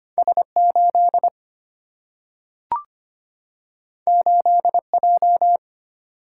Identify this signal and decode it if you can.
Morse code, heard as a keyed tone, about 25 wpm, 700 Hz. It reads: S8 8J
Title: Single-Letter Single-Number 25wpm